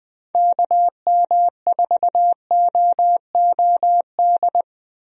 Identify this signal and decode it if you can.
Morse code, heard as a keyed tone, about 20 wpm, 700 Hz. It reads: KM4OOD
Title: USA Call Signs - 6 Characters - Repeated 3x - 20wpm